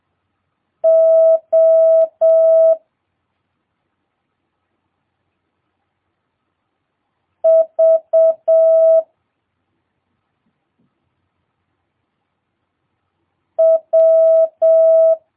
Morse code, heard as a keyed tone, about 7 wpm, 650 Hz. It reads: O V W
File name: jh8cbh / 2022cw